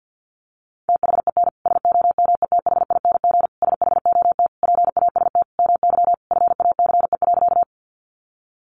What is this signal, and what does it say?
Morse code, heard as a keyed tone, about 50 wpm, 700 Hz. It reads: THEN SOMETHING SHOT PAST MY FACE.
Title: Sentences from Top 1100 Words 50wpm